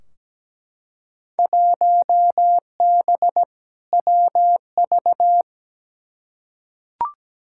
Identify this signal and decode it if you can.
Morse code, heard as a keyed tone, about 17 wpm, 700 Hz. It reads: 1B WV
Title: Field Day Exchanges Encore - 17wpm